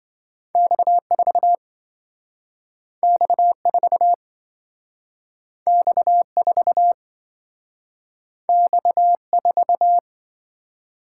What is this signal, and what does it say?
Morse code, heard as a keyed tone, about 26 wpm, 700 Hz. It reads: X4 X4 X4 X4